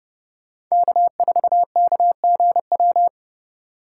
Morse code, heard as a keyed tone, about 36 wpm, 700 Hz. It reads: K4KGW
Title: Speed Racing - USA 1x3 Call Signs 30wpm